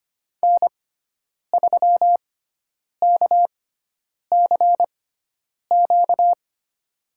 Morse code, heard as a keyed tone, about 25 wpm, spaced as for 9 wpm, 700 Hz. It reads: N3KCQ
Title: USA Call Signs - 5 Characters - 6x Character Spacing - 25wpm